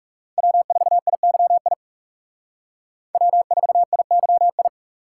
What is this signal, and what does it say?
Morse code, heard as a keyed tone, about 36 wpm, 700 Hz. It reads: W4IYI W4IYI
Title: Speed Racing - USA Call Signs 30wpm